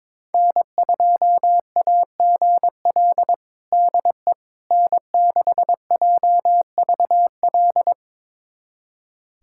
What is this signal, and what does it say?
Morse code, heard as a keyed tone, about 22 wpm, 700 Hz. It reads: N2AGL DE N6JVL
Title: USA Callsign DE Callsign - 5 Characters - 22wpm